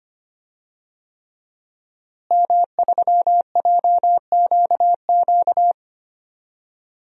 Morse code, heard as a keyed tone, about 25 wpm, 700 Hz. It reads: M3JQQ
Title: DXCC-100 Call Signs - 5 Characters - Repeated 3x - 25wpm